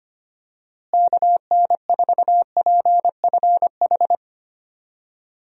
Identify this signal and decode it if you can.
Morse code, heard as a keyed tone, about 25 wpm, 700 Hz. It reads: KN4PFH
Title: USA Call Signs - 6 Characters - Repeated 3x - 25wpm